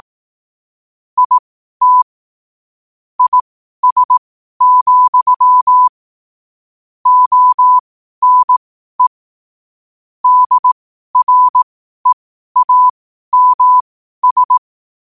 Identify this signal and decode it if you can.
Morse code, heard as a keyed tone, about 18 wpm, spaced as for 12 wpm, 1000 Hz. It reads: IT IS, ONE DREAMS
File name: qotdfiles / 12wpm